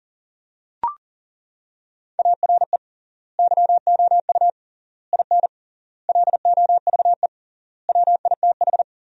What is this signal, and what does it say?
Morse code, heard as a keyed tone, about 40 wpm, 700 Hz. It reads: ARE YOU IN LOVE WITH
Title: 3x Word-Spacing - Sentences from Top 400 Words - 40wpm